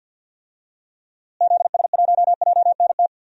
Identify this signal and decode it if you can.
Morse code, heard as a keyed tone, about 51 wpm, 700 Hz. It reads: ZS1JNT
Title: DXCC-100 Call Signs - 6 Characters - Repeated 3x - 50wpm